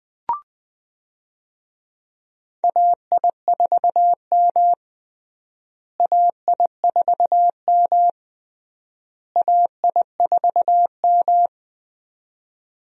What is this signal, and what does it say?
Morse code, heard as a keyed tone, about 20 wpm, 700 Hz. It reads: AI4M AI4M AI4M
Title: USA Call Signs - 4 Characters - Repeated 3x - 20wpm